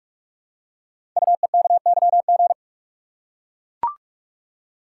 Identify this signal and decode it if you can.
Morse code, heard as a keyed tone, about 44 wpm, 700 Hz. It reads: UEKYG
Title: Code Groups - 5 Letters - 45wpm